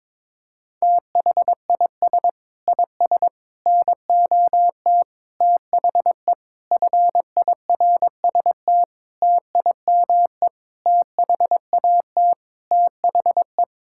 Morse code, heard as a keyed tone, about 22 wpm, 700 Hz. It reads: THIS IS NOT THE FIRST TIME THAT THE
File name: practice / 22wpm / Sets of 8 Words - 22wpm